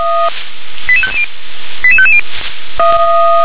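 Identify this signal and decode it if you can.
Five-tone selective call (ZVEI), heard as quick runs of five tones, tones of about 70 ms. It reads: 88550 88550
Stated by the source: ZVEI five-tone call received on SDR.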